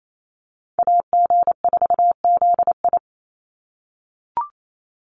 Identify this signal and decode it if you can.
Morse code, heard as a keyed tone, about 28 wpm, 700 Hz. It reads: AG4ZI
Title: USA Call Signs 28wpm